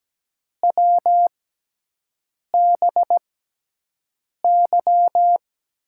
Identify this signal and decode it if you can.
Morse code, heard as a keyed tone, about 17 wpm, spaced as for 6 wpm, 700 Hz. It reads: WBY
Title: USA Call Signs - 5 Characters - 6x Character Spacing - 17wpm